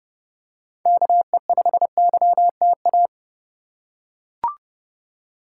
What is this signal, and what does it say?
Morse code, heard as a keyed tone, about 30 wpm, 700 Hz. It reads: KE5YTA